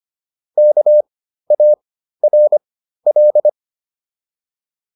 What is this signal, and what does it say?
Morse code, heard as a keyed tone, about 25 wpm, spaced as for 13 wpm, 600 Hz. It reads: KARL